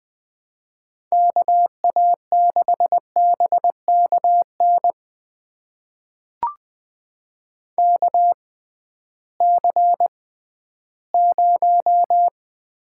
Morse code, heard as a keyed tone, about 20 wpm, spaced as for 13 wpm, 700 Hz. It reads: KA6BKN KC0
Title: USA Call Signs - 6 Characters - 6x Character Spacing - 20wpm